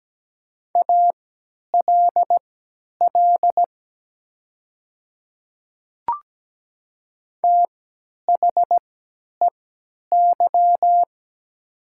Morse code, heard as a keyed tone, about 17 wpm, spaced as for 10 wpm, 700 Hz. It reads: ALL THEY
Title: 3x Character Spacing - Top 200 Words - 17wpm